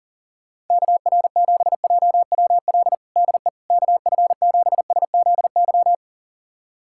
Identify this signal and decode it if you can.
Morse code, heard as a keyed tone, about 40 wpm, 700 Hz. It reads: KR7JWL DE KF7SZY
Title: USA Callsign DE Callsign - 6 Characters - 40wpm